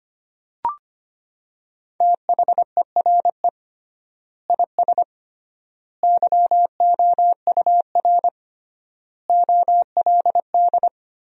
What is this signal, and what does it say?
Morse code, heard as a keyed tone, about 25 wpm, 700 Hz. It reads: THERE IS YOUR OLD
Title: 3x Word-Spacing - Sentences from Top 500 Words - 25wpm